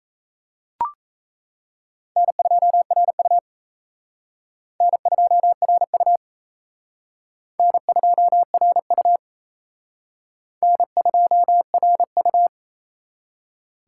Morse code, heard as a keyed tone, about 36 wpm, 700 Hz. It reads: N2RU N2RU N2RU N2RU